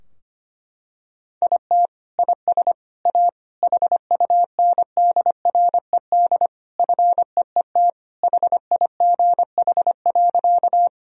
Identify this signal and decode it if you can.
Morse code, heard as a keyed tone, about 25 wpm, 700 Hz. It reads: IT IS A HUNDRED FEET HIGH.
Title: Sentences from Top 1000 Words - Review Part 2 - 25wpm